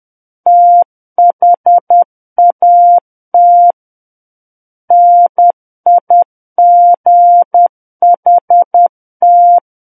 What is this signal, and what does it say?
Morse code, heard as a keyed tone, about 10 wpm, 700 Hz. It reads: THAT NIGHT
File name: Book One / 01-Chapter 1 - 10 WPM